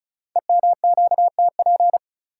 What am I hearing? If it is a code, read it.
Morse code, heard as a keyed tone, about 33 wpm, 690 Hz. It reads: EMQTP
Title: Code Groups - 5 Letters - 35wpm